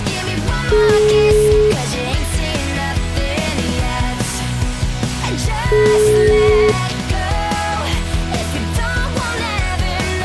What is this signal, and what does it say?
Ringback tone (Europe, 425 Hz), with noise.